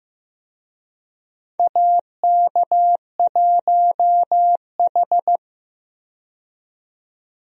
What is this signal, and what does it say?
Morse code, heard as a keyed tone, about 15 wpm, 700 Hz. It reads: AK1H
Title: USA Call Signs - 4 Characters - 15wpm